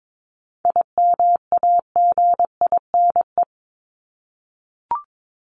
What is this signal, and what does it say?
Morse code, heard as a keyed tone, about 22 wpm, 700 Hz. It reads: IMAGINE
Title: Top 1251-1500 Words 22wpm